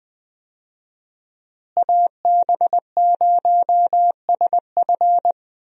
Morse code, heard as a keyed tone, about 20 wpm, 700 Hz. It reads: AB0SF
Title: USA Call Signs - 5 Characters - Repeated 3x - 20wpm